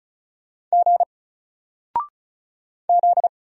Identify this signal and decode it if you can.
Morse code, heard as a keyed tone, about 36 wpm, 700 Hz. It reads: G Z
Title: Single Letter-Number 35wpm